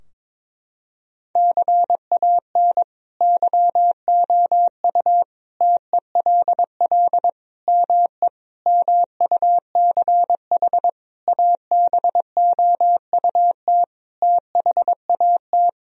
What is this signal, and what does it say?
Morse code, heard as a keyed tone, about 22 wpm, 700 Hz. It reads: CAN YOU TELL ME MUCH ABOUT THAT
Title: Sentences from Top 1000 Words - Review Part 4 - 22wpm